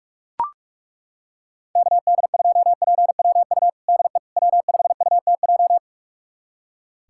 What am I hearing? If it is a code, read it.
Morse code, heard as a keyed tone, about 43 wpm, 700 Hz. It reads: KD2PWU DE W5UTJ
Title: USA Callsign DE Callsign 45wpm